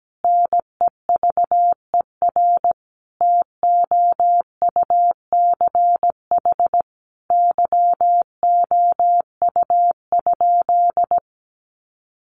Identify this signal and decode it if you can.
Morse code, heard as a keyed tone, about 17 wpm, 700 Hz. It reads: NEVER TOUCH YOU?
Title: Sentences from Top 1200 Words 17wpm